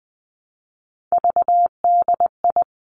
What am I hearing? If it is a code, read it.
Morse code, heard as a keyed tone, about 20 wpm, 700 Hz. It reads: VDI
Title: Three Challenging Letter-Number Combinations 20wpm